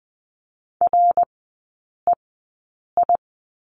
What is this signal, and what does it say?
Morse code, heard as a keyed tone, about 20 wpm, spaced as for 8 wpm, 700 Hz. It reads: REI
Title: 053 Lesson 10e - Letter r - Three Characters 20wpm